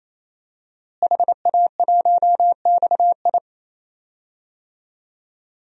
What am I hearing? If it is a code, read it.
Morse code, heard as a keyed tone, about 28 wpm, 700 Hz. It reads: HA1XI
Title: DXCC-100 Call Signs - 5 Characters - Repeated 3x - 28wpm